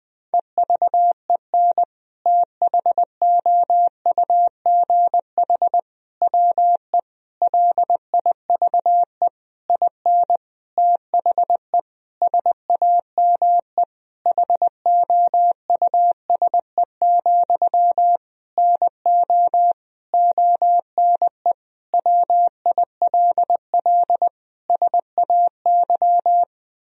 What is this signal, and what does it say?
Morse code, heard as a keyed tone, about 20 wpm, 700 Hz. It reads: EVEN THOUGH WE LIVE IN THE SAME HOUSE, NO ONE WILL SAY
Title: Sentences from Top 1000 Words - Review Part 2 - 20wpm